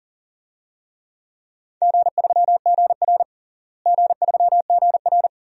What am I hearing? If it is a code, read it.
Morse code, heard as a keyed tone, about 40 wpm, 700 Hz. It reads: G3GR G3GR
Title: DXCC-100 Call Signs - 4 Characters - Repeated 2x - 40wpm